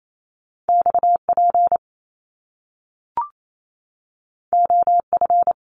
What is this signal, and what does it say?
Morse code, heard as a keyed tone, about 28 wpm, 700 Hz. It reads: XP OF
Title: Top Two Letter Combinations in Words 28wpm